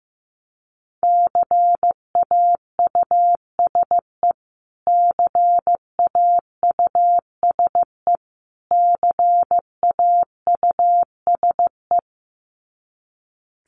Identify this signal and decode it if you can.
Morse code, heard as a keyed tone, about 15 wpm, 700 Hz. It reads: CAUSE CAUSE CAUSE
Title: Top 300 Words Repeated 15wpm